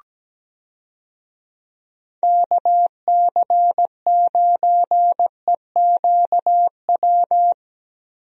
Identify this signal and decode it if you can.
Morse code, heard as a keyed tone, about 17 wpm, 700 Hz. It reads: KC9EQW
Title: USA Call Signs - 6 Characters - Repeated 2x - 17wpm